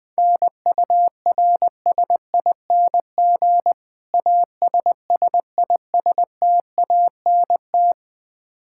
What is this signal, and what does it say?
Morse code, heard as a keyed tone, about 20 wpm, 700 Hz. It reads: NURSING ASSISTANT
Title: Practice QSO - Exchange 3 - More Info - 20wpm